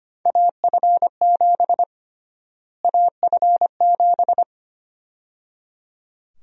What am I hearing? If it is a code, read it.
Morse code, heard as a keyed tone, about 25 wpm, 700 Hz. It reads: AF7 AF7
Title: USA Call Sign Prefixes - Repeated 2x - 25wpm